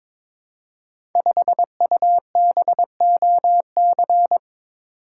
Morse code, heard as a keyed tone, about 22 wpm, 700 Hz. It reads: 5UBOC
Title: Code Groups - 5 Letter:Numbers - 22wpm